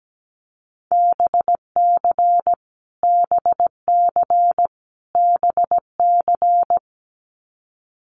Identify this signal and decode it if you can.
Morse code, heard as a keyed tone, about 17 wpm, 700 Hz. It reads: BC BC BC
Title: Canadian Provinces and Territories - Repeated 3x - 17wpm